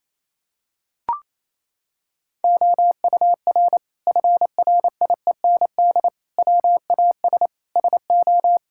Morse code, heard as a keyed tone, about 28 wpm, 700 Hz. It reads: OUR FRIEND WAS SO
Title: Sentences from Top 1000 Words - Review Part 5 - 28wpm